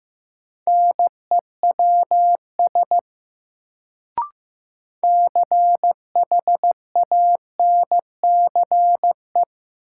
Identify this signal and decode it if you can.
Morse code, heard as a keyed tone, about 15 wpm, 700 Hz. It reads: NEWS CHANCE
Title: Sentences from Top 800 Words - New Word Drill - 15wpm